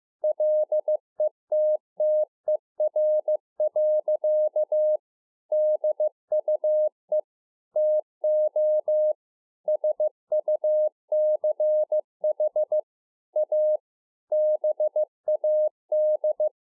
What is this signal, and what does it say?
Morse code, heard as a keyed tone, about 15 wpm, 610 Hz. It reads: LETTER. DUE TO SUCH A BAD